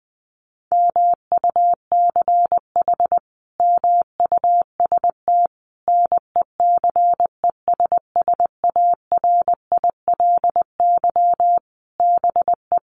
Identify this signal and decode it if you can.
Morse code, heard as a keyed tone, about 20 wpm, 700 Hz. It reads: MUCH MUST NECESSARILY BE
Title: Sentences from Top 1400 Words 20wpm